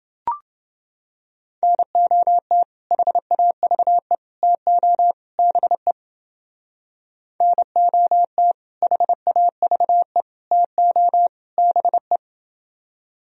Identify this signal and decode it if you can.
Morse code, heard as a keyed tone, about 28 wpm, 700 Hz. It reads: NOT HAVE TO BE NOT HAVE TO BE